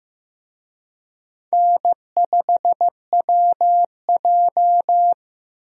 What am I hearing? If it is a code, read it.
Morse code, heard as a keyed tone, about 16 wpm, 700 Hz. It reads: N5WJ
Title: USA Call Signs - 4 Characters - Repeated 2x - 15wpm